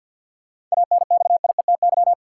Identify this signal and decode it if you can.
Morse code, heard as a keyed tone, about 50 wpm, 700 Hz. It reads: ANXIETY
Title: Sentences from Top 1600 Words - New Word Drill - 50wpm